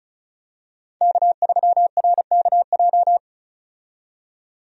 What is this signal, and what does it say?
Morse code, heard as a keyed tone, about 34 wpm, 700 Hz. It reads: K3RKJ
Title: USA Call Signs - 5 Characters - 35wpm